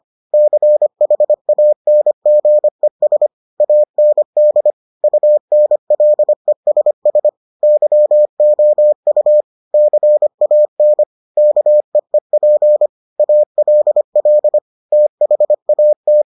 Morse code, heard as a keyed tone, about 25 wpm, 600 Hz. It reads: CHANGES AND UNLESS YOU CAN KEEP ALL THAT